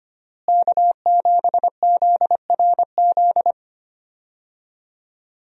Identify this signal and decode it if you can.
Morse code, heard as a keyed tone, about 25 wpm, 700 Hz. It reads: K7ZRZ
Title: USA Call Signs - 5 Characters - Repeated 2x - 25wpm